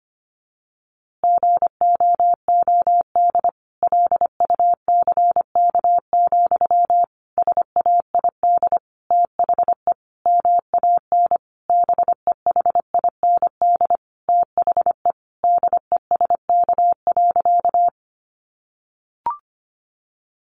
Morse code, heard as a keyed tone, about 25 wpm, 700 Hz. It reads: GOOD LUCK, SAID THE MAN BEHIND THE DESK.
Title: A Sound Of Thunder 25wpm